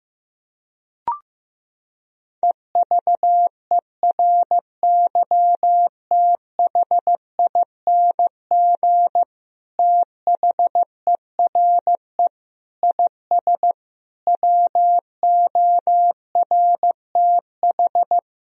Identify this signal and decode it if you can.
Morse code, heard as a keyed tone, about 16 wpm, 700 Hz. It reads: EVERYTHING THERE IS WORTH
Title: Sentences from Top 1000 Words - Review Part 1 - 15wpm